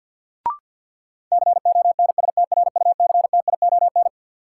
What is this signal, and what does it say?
Morse code, heard as a keyed tone, about 47 wpm, 700 Hz. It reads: CONSTRUCTION
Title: Sentences from Top 1600 Words - New Word Drill Alt - 50wpm